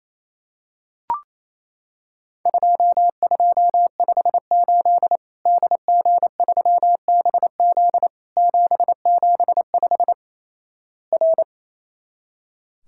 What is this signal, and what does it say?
Morse code, heard as a keyed tone, about 28 wpm, 700 Hz. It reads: 2258 DG3BZ 775 R
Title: Worked All Europe DX Contest - QTC Exchange - 28wpm